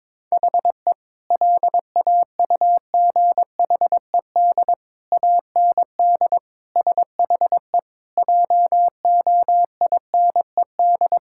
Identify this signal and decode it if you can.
Morse code, heard as a keyed tone, about 22 wpm, 700 Hz. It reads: HE LAUGHED AND SHE JOINED
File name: practice / 22wpm / Sentences with 2-Word Phrases - 22wpm